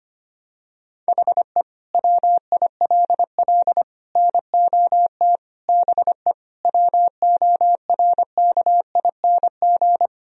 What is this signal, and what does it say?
Morse code, heard as a keyed tone, about 25 wpm, 700 Hz. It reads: HE WILL NOT BE WORKING
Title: Sentences from Top 1000 Words - Review Part 4 - 25wpm